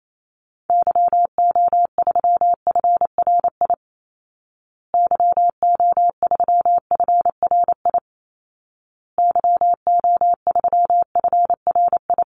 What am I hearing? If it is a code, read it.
Morse code, heard as a keyed tone, about 28 wpm, 700 Hz. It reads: YO3FRI YO3FRI YO3FRI
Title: Call Signs Repeated 2x - 28wpm